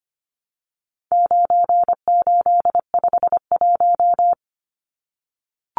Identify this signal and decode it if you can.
Morse code, heard as a keyed tone, about 25 wpm, 700 Hz. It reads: 9851
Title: Numbers - 4-Digit 25wpm